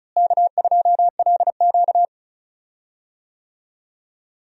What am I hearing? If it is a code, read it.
Morse code, heard as a keyed tone, about 36 wpm, 700 Hz. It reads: K2LQ
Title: USA Call Signs - 4 Characters - Repeated 3x - 35wpm